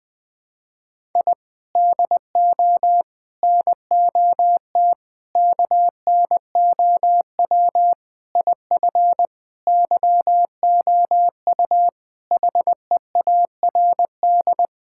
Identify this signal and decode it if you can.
Morse code, heard as a keyed tone, about 20 wpm, 700 Hz. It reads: I DO NOT KNOW IF YOU HEARD
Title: Sentences with 5-Word Phrases - 20wpm